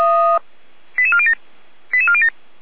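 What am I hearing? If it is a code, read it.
Five-tone selective call (ZVEI), heard as quick runs of five tones, tones of about 70 ms. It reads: 88407 88407
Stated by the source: Five-tone ZVEI selective call received on an SDR, fire brigade alerting.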